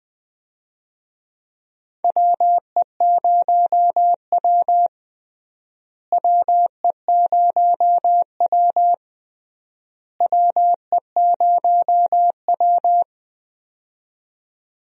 Morse code, heard as a keyed tone, about 21 wpm, 700 Hz. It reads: WE0W WE0W WE0W
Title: USA Call Signs - 4 Characters - Repeated 3x - 20wpm